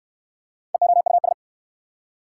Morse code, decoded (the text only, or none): EOUS